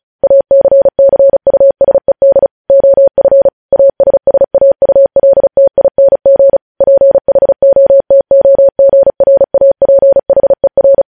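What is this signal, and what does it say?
Morse code, heard as a keyed tone, about 36 wpm, 550 Hz. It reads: ACCUSED OF ASSAULTING PHOTOGRAPHER
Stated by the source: Morse code practice reading news headlines at 35 words per minute.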